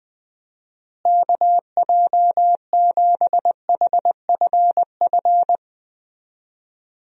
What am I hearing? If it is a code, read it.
Morse code, heard as a keyed tone, about 20 wpm, 700 Hz. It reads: KJ7HFF
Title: USA Call Signs - 6 Characters - Repeated 3x - 20wpm